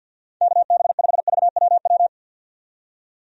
Morse code, heard as a keyed tone, about 52 wpm, 700 Hz. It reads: KB5VWW
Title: USA Call Signs - 6 Characters - 50wpm